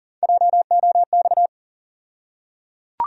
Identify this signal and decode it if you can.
Morse code, heard as a keyed tone, about 40 wpm, 700 Hz. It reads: JOX
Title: USA Call Sign Postfixes - 3 Letters 40wpm